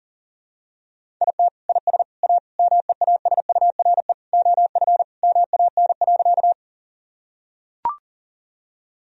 Morse code, heard as a keyed tone, about 40 wpm, 700 Hz. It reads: IT IS A MEASURE OF MAN.
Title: Sentences from Top 1000 Words - Review Part 2 - 40wpm